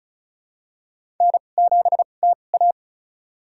Morse code, heard as a keyed tone, about 36 wpm, spaced as for 25 wpm, 700 Hz. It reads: N7TA
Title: USA Call Signs - 2x Character Spacing - 35wpm